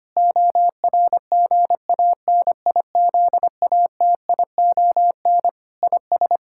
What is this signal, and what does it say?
Morse code, heard as a keyed tone, about 25 wpm, 700 Hz. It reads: ORGANIZATION IS
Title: Sentences from Top 1000 Words - Review Part 5 - 25wpm